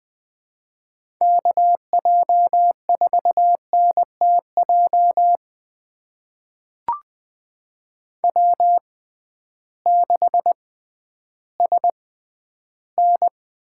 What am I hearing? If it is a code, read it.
Morse code, heard as a keyed tone, about 20 wpm, spaced as for 12 wpm, 700 Hz. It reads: KJ4NTJ W6SN
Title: USA Call Signs - 6x Character Spacing - 20wpm